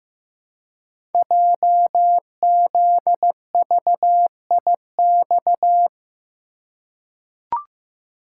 Morse code, heard as a keyed tone, about 15 wpm, 700 Hz. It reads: JZVIX
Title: Code Groups - 5 Letters - 15wpm